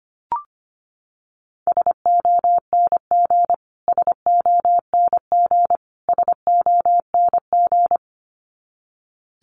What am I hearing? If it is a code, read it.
Morse code, heard as a keyed tone, about 25 wpm, 700 Hz. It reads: SONG SONG SONG
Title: Top 500 Words Repeated 25wpm